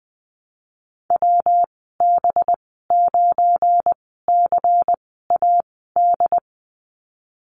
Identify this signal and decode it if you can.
Morse code, heard as a keyed tone, about 19 wpm, spaced as for 14 wpm, 700 Hz. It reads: WB9CAD
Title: 193 Lesson 31g - Letter v - Call Signs 20wpm